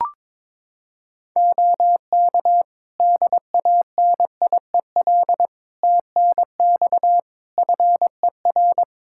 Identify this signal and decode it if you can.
Morse code, heard as a keyed tone, about 22 wpm, 700 Hz. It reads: OK DANIEL TNX FER
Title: Practice QSO - Exchange 4 - Ending - 22wpm